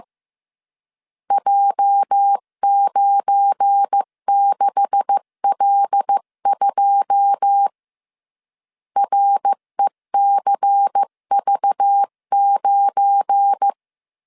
Morse code, heard as a keyed tone, about 15 wpm, 800 Hz. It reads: J96L2 RECV9
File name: CW / DET / 1 9 1500C S B 14wpm 3302